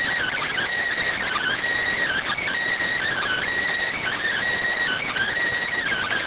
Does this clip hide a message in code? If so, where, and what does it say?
Yes: Emergency Alert System (SAME) tones from 1.2 to 5.3 s, heard as harsh buzzing bursts.